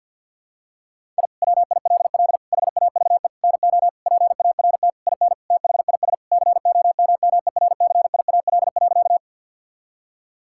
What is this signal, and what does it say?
Morse code, heard as a keyed tone, about 46 wpm, 700 Hz. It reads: I WILL HAVE NO PART IN THIS COMMERCIAL.
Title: Sentences from Top 1000 Words - Review Part 5 - 50wpm